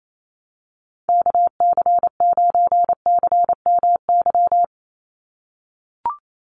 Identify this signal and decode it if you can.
Morse code, heard as a keyed tone, about 28 wpm, 700 Hz. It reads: KC9CMY
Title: USA Call Signs 28wpm